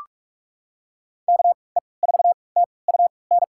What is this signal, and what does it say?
Morse code, heard as a keyed tone, about 44 wpm, spaced as for 25 wpm, 700 Hz. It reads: KE4TUN
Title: USA Call Signs - 6 Characters - 3x Character Spacing - 45wpm